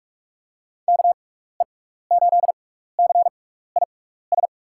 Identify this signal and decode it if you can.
Morse code, heard as a keyed tone, about 47 wpm, spaced as for 16 wpm, 700 Hz. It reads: KE8CIS
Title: USA Call Signs - 6 Characters - 6x Character Spacing - 45wpm